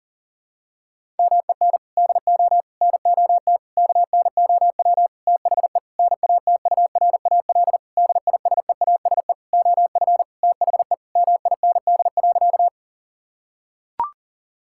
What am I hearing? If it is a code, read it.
Morse code, heard as a keyed tone, about 37 wpm, 700 Hz. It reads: MEN DO NOT KNOW THE NATURAL DISEASE OF THE MIND.
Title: Sentences from Top 1000 Words - Review Part 6 - 40wpm